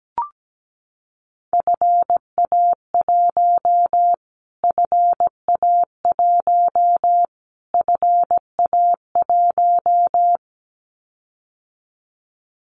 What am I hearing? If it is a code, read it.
Morse code, heard as a keyed tone, about 17 wpm, 700 Hz. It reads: FA1 FA1 FA1
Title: Callsign Prefixes for Top 25 DXCC - Repeated 3x - 17wpm